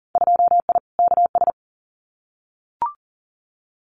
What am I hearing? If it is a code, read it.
Morse code, heard as a keyed tone, about 40 wpm, 700 Hz. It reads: 2I KS
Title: Winter Field Day Exchanges Encore 40wpm